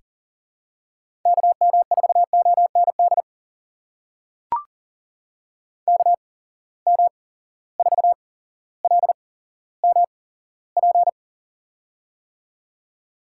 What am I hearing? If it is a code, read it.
Morse code, heard as a keyed tone, about 40 wpm, spaced as for 17 wpm, 700 Hz. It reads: KM4OND KM4LMP
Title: USA Call Signs - 6 Characters - 8x Character Spacing - 40wpm